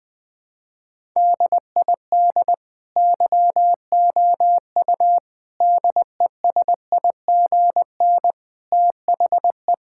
Morse code, heard as a keyed tone, about 20 wpm, 700 Hz. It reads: DID YOU DESIGN THE
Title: Sentences from Top 1000 Words - Review Part 5 - 20wpm